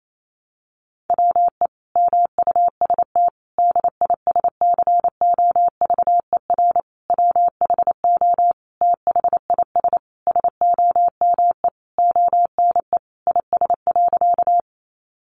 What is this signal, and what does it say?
Morse code, heard as a keyed tone, about 28 wpm, 700 Hz. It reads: WE MUST DISCOVER WHO THIS SOME ONE IS.
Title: Sentences from Top 1100 Words 28wpm